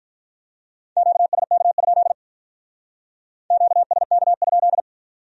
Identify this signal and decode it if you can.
Morse code, heard as a keyed tone, about 43 wpm, 700 Hz. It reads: QSK? QSK?
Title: Speed-Racing Q-Codes - 35wpm